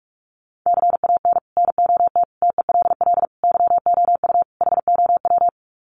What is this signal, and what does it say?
Morse code, heard as a keyed tone, about 44 wpm, 700 Hz. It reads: CAN NOT TELL YOU HOW
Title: Sets of 5 Words Encore2 45wpm